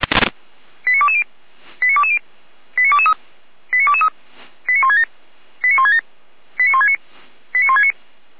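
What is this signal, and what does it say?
Five-tone selective call (ZVEI), heard as quick runs of five tones, tones of about 70 ms. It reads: 89220 89220 89223 89223 89167 89167 89169 89169
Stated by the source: ZVEI five-tone selective call received by SDR.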